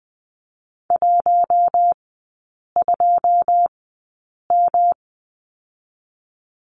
Morse code, heard as a keyed tone, about 20 wpm, 700 Hz. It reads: 1 2 M
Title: 121 Lesson 20e - Number 3 - Three Characters 20wpm